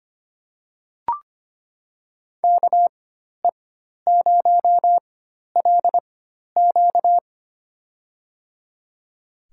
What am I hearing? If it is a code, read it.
Morse code, heard as a keyed tone, about 25 wpm, spaced as for 12 wpm, 700 Hz. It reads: KE0LQ
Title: USA Call Signs - 5 Characters - 4x Character Spacing - 25wpm